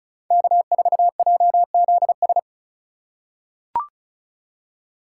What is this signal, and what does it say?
Morse code, heard as a keyed tone, about 36 wpm, 700 Hz. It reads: K4JZS
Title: USA Call Signs - 5 Characters - 4x Character Spacing - 35wpm